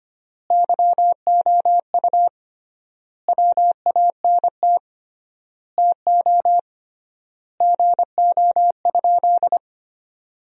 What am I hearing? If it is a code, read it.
Morse code, heard as a keyed tone, about 25 wpm, 700 Hz. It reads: YOU WANT TO GO?
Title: 3x Word-Spacing - Sentences from Top 300 Words - 25wpm